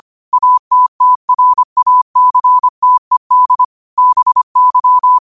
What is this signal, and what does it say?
Morse code, heard as a keyed tone, about 25 wpm, 1000 Hz. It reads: ATTRACTED BY
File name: 25wpm